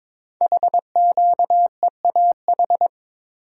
Morse code, heard as a keyed tone, about 22 wpm, 700 Hz. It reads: HQEAH
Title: Code Groups - 5 Letters - 22wpm